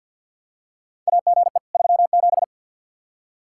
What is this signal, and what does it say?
Morse code, heard as a keyed tone, about 47 wpm, 700 Hz. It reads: AGE 37
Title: Practice QSO - Exchange 3 - More Info - 50wpm